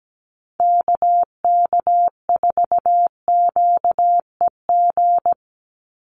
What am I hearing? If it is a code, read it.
Morse code, heard as a keyed tone, about 17 wpm, 700 Hz. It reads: KK4QEG
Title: USA Call Signs 17wpm